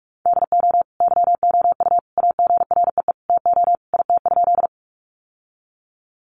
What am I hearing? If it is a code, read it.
Morse code, heard as a keyed tone, about 44 wpm, 700 Hz. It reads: DO YOU AGREE TO IT?
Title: Sentences from Top 1100 Words 45wpm